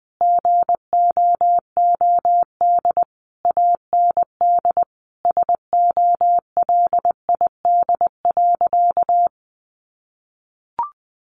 Morse code, heard as a keyed tone, about 20 wpm, 700 Hz. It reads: GOOD AND SOLID.
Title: Sentences from Top 1500 Words 20wpm